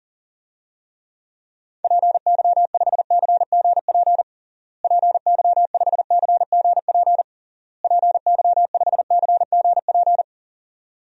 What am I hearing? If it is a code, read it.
Morse code, heard as a keyed tone, about 40 wpm, 700 Hz. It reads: PY5CGP PY5CGP PY5CGP
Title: DXCC-100 Call Signs - 6 Characters - Repeated 3x - 40wpm